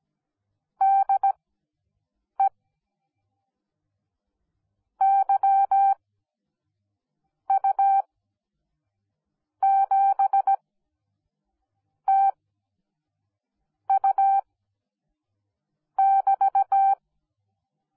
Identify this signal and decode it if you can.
Morse code, heard as a keyed tone, about 17 wpm, spaced as for 5 wpm, 780 Hz. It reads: DE YU7TU<BT>
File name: Onlinelehrgang / morsen / DJ4UF Morsen10